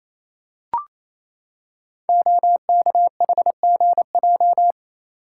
Morse code, heard as a keyed tone, about 28 wpm, 700 Hz. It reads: OKHGJ
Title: Code Groups - 5 Letter:Numbers - 28wpm